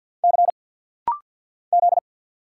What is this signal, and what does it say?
Morse code, heard as a keyed tone, about 50 wpm, 690 Hz. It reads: C Z
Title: Single Letters - Rapid-Fire 50wpm